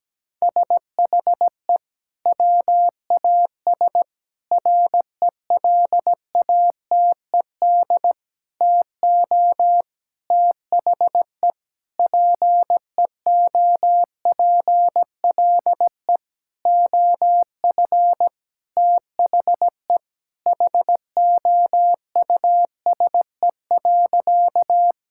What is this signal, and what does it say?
Morse code, heard as a keyed tone, about 17 wpm, 700 Hz. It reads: SHE WAS RELATED TO THE PEOPLE OF THE HOUSE.
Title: Sentences from Top 1000 Words - Review Part 5 - 17wpm